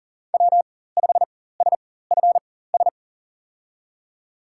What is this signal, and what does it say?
Morse code, heard as a keyed tone, about 40 wpm, spaced as for 19 wpm, 700 Hz. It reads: W5SFS
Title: USA Call Signs - 5 Characters - 4x Character Spacing - 40wpm